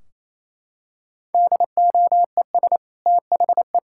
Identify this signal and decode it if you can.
Morse code, heard as a keyed tone, about 28 wpm, 700 Hz. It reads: DOES THE
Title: Sentences from Top 1600 Words 28wpm